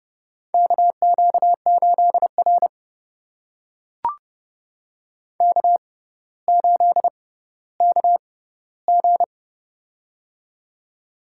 Morse code, heard as a keyed tone, about 30 wpm, spaced as for 15 wpm, 700 Hz. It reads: KQ8R K8KG
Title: USA Call Signs - 4 Characters - 6x Character Spacing - 30wpm